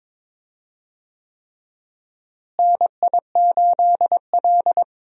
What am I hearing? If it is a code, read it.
Morse code, heard as a keyed tone, about 22 wpm, 700 Hz. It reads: NI8L
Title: USA Call Signs - 4 Characters - Repeated 3x - 22wpm